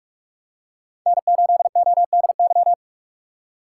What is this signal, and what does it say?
Morse code, heard as a keyed tone, about 46 wpm, 700 Hz. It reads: N8ODY
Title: USA Call Signs - 5 Characters - 2x Character Spacing - 45wpm